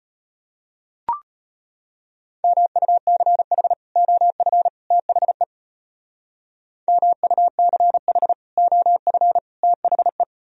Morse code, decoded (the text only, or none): MUCH OF THE MUCH OF THE